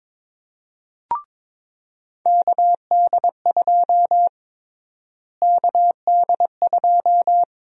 Morse code, heard as a keyed tone, about 22 wpm, 700 Hz. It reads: KD2 KD2
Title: USA Call Sign Prefixes - Repeated 2x - 22wpm